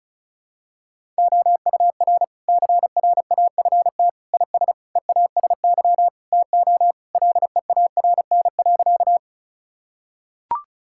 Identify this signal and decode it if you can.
Morse code, heard as a keyed tone, about 36 wpm, 700 Hz. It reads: OUR CRAFT IS EASY TO LEARN.
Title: Sentences from Top 1000 Words - Review Part 6 - 35wpm